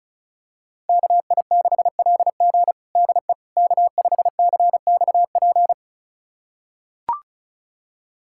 Morse code, heard as a keyed tone, about 36 wpm, 700 Hz. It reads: KI6LG DE K5CXP
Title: USA Callsign DE Callsign - 5 Characters - 35wpm